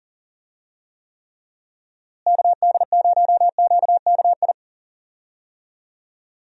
Morse code, heard as a keyed tone, about 40 wpm, 700 Hz. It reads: KD0QKI
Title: USA Call Signs - 6 Characters - Repeated 2x - 40wpm